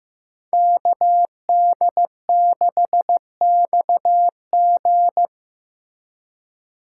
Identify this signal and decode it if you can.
Morse code, heard as a keyed tone, about 18 wpm, 700 Hz. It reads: KD6XG
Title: Speed Racing - USA 2x2 Call Signs 15wpm